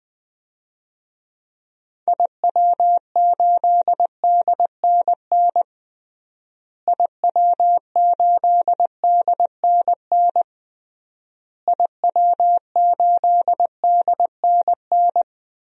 Morse code, heard as a keyed tone, about 20 wpm, 700 Hz. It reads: IW8DNN IW8DNN IW8DNN
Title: DXCC-100 Call Signs - 6 Characters - Repeated 3x - 20wpm